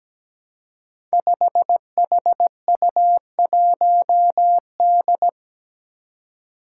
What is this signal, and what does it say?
Morse code, heard as a keyed tone, about 17 wpm, 700 Hz. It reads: 5HU1D
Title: Five Challenging Letter-Number Combinations 17wpm